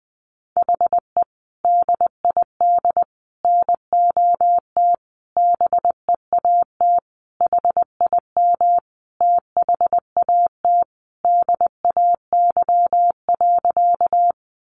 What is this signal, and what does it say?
Morse code, heard as a keyed tone, about 19 wpm, 700 Hz. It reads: HE DID NOT BEAT HIM THAT DAY.
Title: Sentences from Top 1300 Words 20wpm